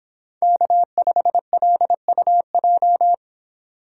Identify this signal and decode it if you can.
Morse code, heard as a keyed tone, about 26 wpm, 700 Hz. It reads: K5LUJ